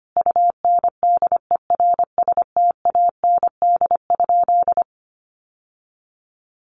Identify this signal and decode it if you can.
Morse code, heard as a keyed tone, about 25 wpm, 700 Hz. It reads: UNDERSTAND?
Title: Sentences from Top 1100 Words 25wpm